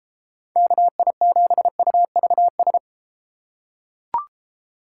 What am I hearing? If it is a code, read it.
Morse code, heard as a keyed tone, about 31 wpm, 700 Hz. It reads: KI7UVS